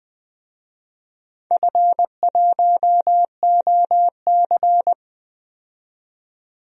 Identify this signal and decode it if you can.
Morse code, heard as a keyed tone, about 20 wpm, 700 Hz. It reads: F1OC
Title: DXCC-100 Call Signs - 4 Characters - 20wpm